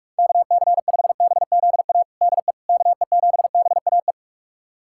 Morse code, heard as a keyed tone, about 42 wpm, 700 Hz. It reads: KC5BZA DE KE7BAE
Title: USA Callsign DE Callsign 45wpm